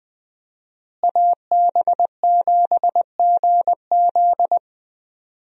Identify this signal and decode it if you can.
Morse code, heard as a keyed tone, about 20 wpm, 700 Hz. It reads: AB7GZ
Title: USA Call Signs - 5 Characters - Repeated 2x - 20wpm